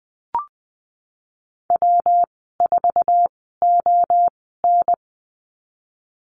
Morse code, heard as a keyed tone, about 20 wpm, spaced as for 15 wpm, 700 Hz. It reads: W4ON
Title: 161 Lesson 26g - Letter g - Call Signs 20wpm